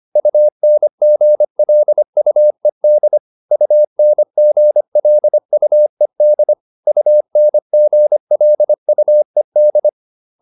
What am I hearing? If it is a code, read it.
Morse code, heard as a keyed tone, about 25 wpm, 600 Hz. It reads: UNGLUED UNGLUED UNGLUED